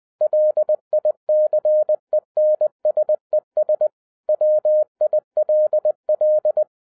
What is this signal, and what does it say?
Morse code, heard as a keyed tone, about 20 wpm, 610 Hz. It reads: LICENSES WILL